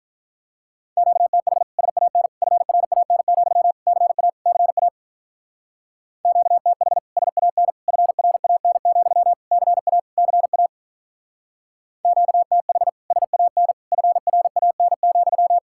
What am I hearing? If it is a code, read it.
Morse code, heard as a keyed tone, about 43 wpm, 700 Hz. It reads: QTH SAN FRAN, CA CA QTH SAN FRAN, CA CA QTH SAN FRAN,
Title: Speed-Racing Common QSO Phrases - 35wpm